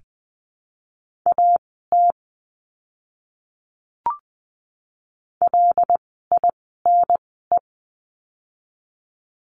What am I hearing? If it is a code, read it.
Morse code, heard as a keyed tone, about 20 wpm, spaced as for 15 wpm, 700 Hz. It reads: AT LINE
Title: 115 Lesson 19f - Letter w - Words 20wpm